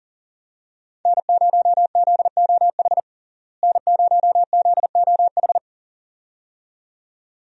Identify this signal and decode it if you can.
Morse code, heard as a keyed tone, about 37 wpm, 700 Hz. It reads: N0ZOH N0ZOH
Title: USA Call Signs - 5 Characters - Repeated 2x - 40wpm